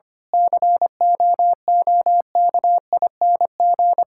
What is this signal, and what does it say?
Morse code, heard as a keyed tone, about 25 wpm, 700 Hz. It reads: COOKING
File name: practice / 25wpm / Difficult Sentences with Hints - 25wpm